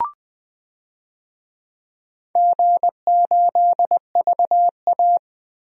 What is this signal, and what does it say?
Morse code, heard as a keyed tone, about 20 wpm, 700 Hz. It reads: G8VA
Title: DXCC-100 Call Signs - 4 Characters - Repeated 2x - 20wpm